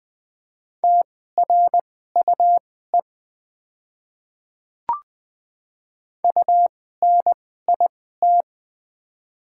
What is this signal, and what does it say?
Morse code, heard as a keyed tone, about 20 wpm, spaced as for 15 wpm, 700 Hz. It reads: TRUE UNIT
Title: 2x Character Spacing - Top 500 Words - 20wpm